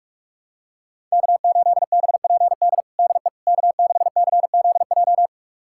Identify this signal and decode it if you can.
Morse code, heard as a keyed tone, about 46 wpm, 700 Hz. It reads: K8BPD DE K6CZJ
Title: USA Callsign DE Callsign - 5 Characters - 45wpm